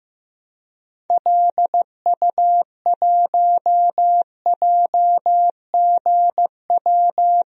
Morse code, heard as a keyed tone, about 15 wpm, 700 Hz. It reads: LU1JGW
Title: DXCC-100 Call Signs - 6 Characters - 15wpm